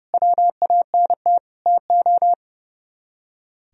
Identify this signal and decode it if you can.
Morse code, heard as a keyed tone, about 39 wpm, 700 Hz. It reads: WANT TO